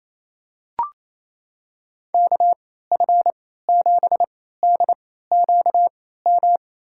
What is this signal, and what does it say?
Morse code, heard as a keyed tone, about 28 wpm, spaced as for 16 wpm, 700 Hz. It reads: KF7DQM
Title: USA Call Signs - 6 Characters - 3x Character Spacing - 28wpm